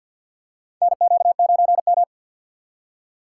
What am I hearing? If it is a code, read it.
Morse code, heard as a keyed tone, about 49 wpm, 700 Hz. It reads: NQ9M
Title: USA Call Signs - 4 Characters - 3x Character Spacing - 50wpm